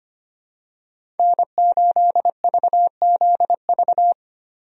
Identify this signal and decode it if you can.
Morse code, heard as a keyed tone, about 25 wpm, 700 Hz. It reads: N8VZV
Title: USA Call Signs - 5 Characters - Repeated 3x - 25wpm